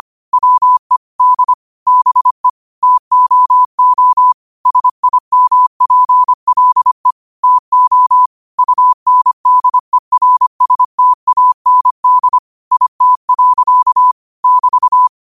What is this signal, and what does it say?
Morse code, heard as a keyed tone, about 25 wpm, 1000 Hz. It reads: WE D BE TOO SIMPLE TO UNDERSTAND IT. <BT>
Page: New quotes every day in morse code at 25 Words per minute.